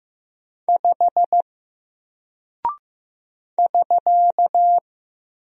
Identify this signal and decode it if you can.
Morse code, heard as a keyed tone, about 15 wpm, 700 Hz. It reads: 5 <SK>
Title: Single Character (Letter:Number:Punctuation:Prosign) 15wpm